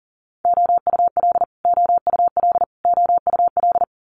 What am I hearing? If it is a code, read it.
Morse code, heard as a keyed tone, about 42 wpm, 700 Hz. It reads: OUL OUL OUL
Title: Multi-letter Phonemes Repeated 3x 40wpm